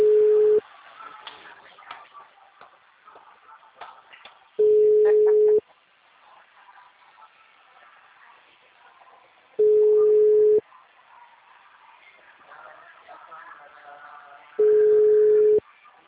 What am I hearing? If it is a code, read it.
Ringback tone (Europe, 425 Hz), with speech.
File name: out- -2019- - - .0